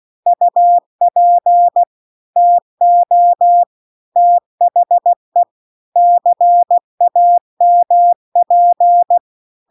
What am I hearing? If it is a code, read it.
Morse code, heard as a keyed tone, about 16 wpm, 700 Hz. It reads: UP TO THE CAMP